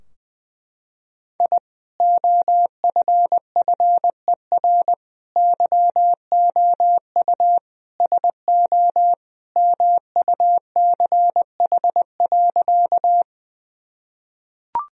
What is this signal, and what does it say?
Morse code, heard as a keyed tone, about 20 wpm, 700 Hz. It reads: I OFFER YOU SO MUCH.
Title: Sentences from Top 1000 Words - Review Part 4 - 20wpm